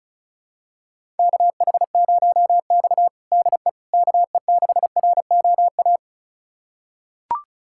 Morse code, heard as a keyed tone, about 36 wpm, 700 Hz. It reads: KH0X DE KE6ROA
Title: USA Callsign DE Callsign 35wpm